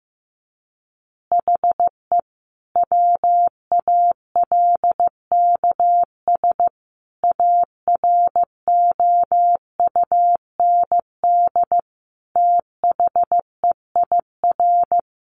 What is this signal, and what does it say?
Morse code, heard as a keyed tone, about 15 wpm, 700 Hz. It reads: HE WALKS AROUND THEIR
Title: Easy Phrases 15wpm